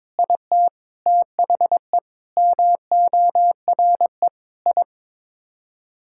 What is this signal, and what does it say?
Morse code, heard as a keyed tone, about 22 wpm, 700 Hz. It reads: IT THE MORE I
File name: Sets of 9 Words - 22wpm